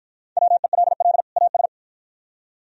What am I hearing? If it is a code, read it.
Morse code, heard as a keyed tone, about 54 wpm, 700 Hz. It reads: WELL AS